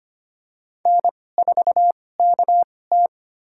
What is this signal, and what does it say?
Morse code, heard as a keyed tone, about 25 wpm, spaced as for 18 wpm, 700 Hz. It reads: N4KT
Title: USA Call Signs - 4 Characters - 2x Character Spacing - 25wpm